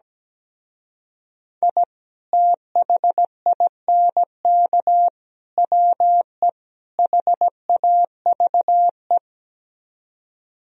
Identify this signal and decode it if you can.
Morse code, heard as a keyed tone, about 17 wpm, 700 Hz. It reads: I THINK WE HAVE
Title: Speed-Racing - Sets of 4 Words 17wpm